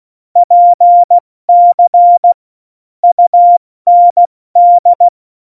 Morse code, heard as a keyed tone, about 16 wpm, 700 Hz. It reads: PC UND
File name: Morsekurs / Witze / Witz 22. Januar 2019, 70 F80